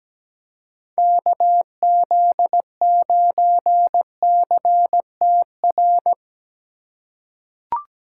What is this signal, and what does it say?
Morse code, heard as a keyed tone, about 17 wpm, 700 Hz. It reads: KZ9CTR
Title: USA Call Signs - 6 Characters - 4x Character Spacing - 17wpm